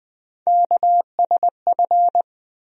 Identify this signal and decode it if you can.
Morse code, heard as a keyed tone, about 20 wpm, 700 Hz. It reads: KSF
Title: USA Call Sign Postfixes - 3 Letters 20wpm